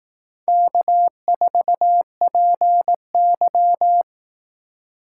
Morse code, heard as a keyed tone, about 18 wpm, 700 Hz. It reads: K4PY
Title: Speed Racing - USA 1x2 Call Signs 15wpm